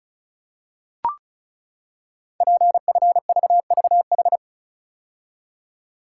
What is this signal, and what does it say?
Morse code, heard as a keyed tone, about 36 wpm, 700 Hz. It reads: PFVVH
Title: Code Groups - 5 Letters - 35wpm